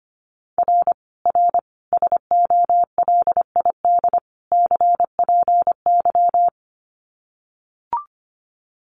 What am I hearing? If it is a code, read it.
Morse code, heard as a keyed tone, about 25 wpm, 700 Hz. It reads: R R SOLID CPY
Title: Common QSO Phrases 25wpm